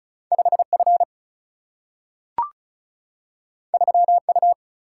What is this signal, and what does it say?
Morse code, heard as a keyed tone, about 36 wpm, 700 Hz. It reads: 5 F 3 U
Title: Single-Letter Single-Number 35wpm